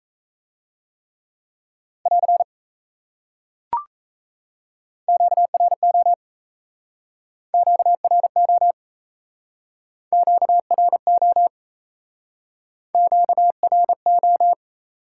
Morse code, heard as a keyed tone, about 37 wpm, 700 Hz. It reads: <AR> QRO QRO QRO QRO